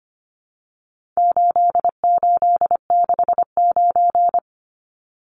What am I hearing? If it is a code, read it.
Morse code, heard as a keyed tone, about 25 wpm, 700 Hz. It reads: 8869
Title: Numbers - 4-Digit 25wpm